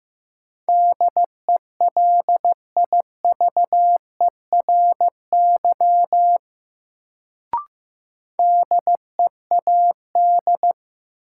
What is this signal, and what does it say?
Morse code, heard as a keyed tone, about 15 wpm, 700 Hz. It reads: DELIVERY DEAD
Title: Sentences from Top 1500 Words - New Word Drill - 15wpm